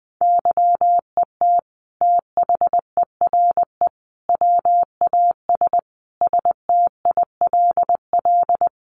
Morse code, heard as a keyed tone, about 20 wpm, 700 Hz. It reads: YET THERE WAS STILL
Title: Sentences from Top 1300 Words 20wpm